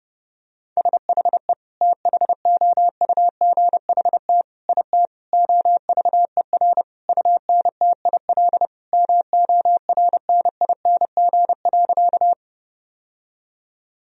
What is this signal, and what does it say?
Morse code, heard as a keyed tone, about 30 wpm, 700 Hz. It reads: SHE THOUGHT IT OVER UNTIL MORNING.
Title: Sentences from Top 1000 Words - Review Part 1 - 30wpm